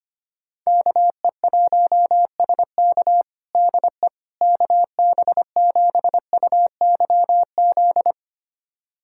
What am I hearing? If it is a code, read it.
Morse code, heard as a keyed tone, about 25 wpm, 700 Hz. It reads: KE1SK DE KB7UYZ
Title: USA Callsign DE Callsign 25wpm